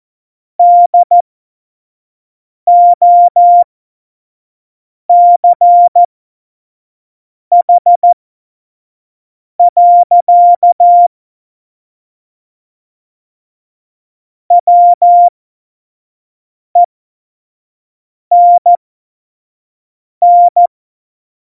Morse code, heard as a keyed tone, about 14 wpm, spaced as for 5 wpm, 700 Hz. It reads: DOCH. WENN
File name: Morsekurs / Witze / Witz 30. Januar 2018, 25 F70